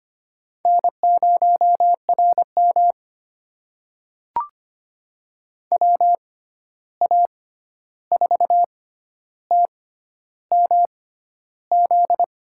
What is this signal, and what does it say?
Morse code, heard as a keyed tone, about 25 wpm, spaced as for 11 wpm, 700 Hz. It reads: N0RM WA4TMZ
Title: USA Call Signs - 6x Character Spacing - 25wpm